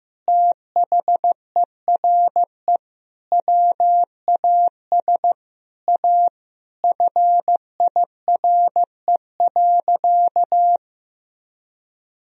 Morse code, heard as a keyed tone, about 15 wpm, 700 Hz. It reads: THERE WAS A FIRE.
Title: Sentences from Top 1000 Words - Review Part 4 - 15wpm